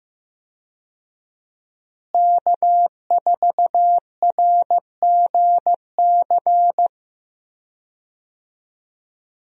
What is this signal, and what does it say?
Morse code, heard as a keyed tone, about 15 wpm, 700 Hz. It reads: K4RGC
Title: USA Call Signs - 5 Characters - Repeated 2x - 15wpm